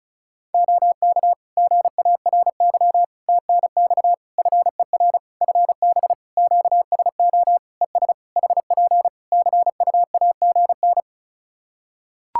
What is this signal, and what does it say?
Morse code, heard as a keyed tone, about 34 wpm, 700 Hz. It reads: OK GARY TNX FER FB QSO ES HP CUAGN
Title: Practice QSO - Exchange 4 - Ending - 35wpm